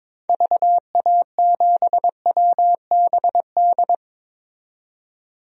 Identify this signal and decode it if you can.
Morse code, heard as a keyed tone, about 22 wpm, 700 Hz. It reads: VA7WBD
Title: DXCC-100 Call Signs - 6 Characters - 22wpm